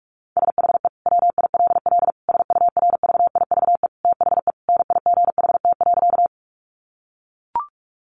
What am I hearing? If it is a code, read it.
Morse code, heard as a keyed tone, about 44 wpm, 700 Hz. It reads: SHE WILL SURVIVE THE NIGHT.
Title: Sentences from Top 1300 Words 45wpm